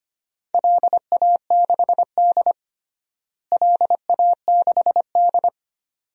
Morse code, heard as a keyed tone, about 25 wpm, 700 Hz. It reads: LA6D LA6D
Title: DXCC-100 Call Signs - 4 Characters - Repeated 2x - 25wpm